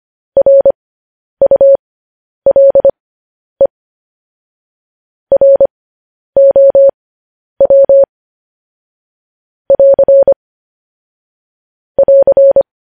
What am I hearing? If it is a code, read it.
Morse code, heard as a keyed tone, about 25 wpm, spaced as for 10 wpm, 550 Hz. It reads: RULE ROW <AR> <AR>
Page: news headlines in morse code at 10 wpm